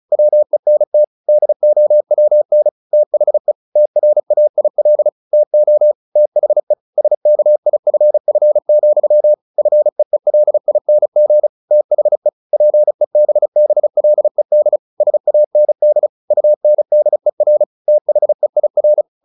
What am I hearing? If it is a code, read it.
Morse code, heard as a keyed tone, about 34 wpm, 600 Hz. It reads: WENT DOWN THE TRAIL TO THE SKIFF, FEELING THE PEBBLED SAND UNDER THEIR
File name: the old man and the sea / the old man and the sea 1 35wpm